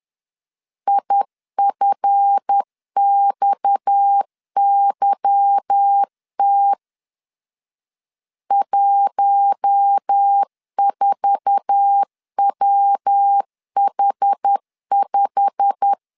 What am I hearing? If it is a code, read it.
Morse code, heard as a keyed tone, about 10 wpm, 800 Hz. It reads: IFXYT 14WH5
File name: LEZIONI / 8 YJ45 / 8 YJ45 10wpm 2549